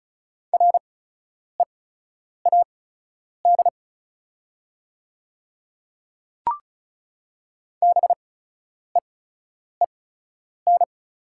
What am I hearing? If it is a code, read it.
Morse code, heard as a keyed tone, about 34 wpm, spaced as for 10 wpm, 700 Hz. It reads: READ BEEN
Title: 8x Character Spacing - Top 300 Words - 35wpm